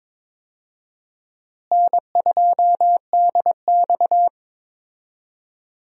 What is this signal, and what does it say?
Morse code, heard as a keyed tone, about 22 wpm, 700 Hz. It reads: N2DX
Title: USA Call Signs - 4 Characters - Repeated 3x - 22wpm